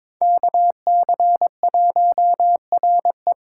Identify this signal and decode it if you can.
Morse code, heard as a keyed tone, about 22 wpm, 700 Hz. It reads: KC1RE
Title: USA Call Signs - 5 Characters - 3x Character Spacing - 22wpm